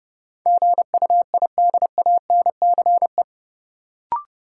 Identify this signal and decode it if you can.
Morse code, heard as a keyed tone, about 30 wpm, 700 Hz. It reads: GUIDANCE
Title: Sentences from Top 1600 Words - New Word Drill - 30wpm